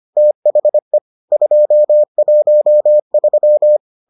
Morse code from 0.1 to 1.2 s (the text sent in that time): THE